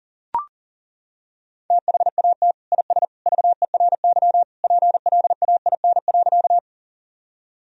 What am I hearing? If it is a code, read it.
Morse code, heard as a keyed tone, about 37 wpm, 700 Hz. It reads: THAT IS VERY PLAIN.
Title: Sentences from Top 1000 Words - Review Part 2 - 40wpm